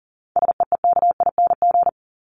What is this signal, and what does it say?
Morse code, heard as a keyed tone, about 40 wpm, 700 Hz. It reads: SEEKING
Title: ING Suffix - Top 100 Words with Base Words 40wpm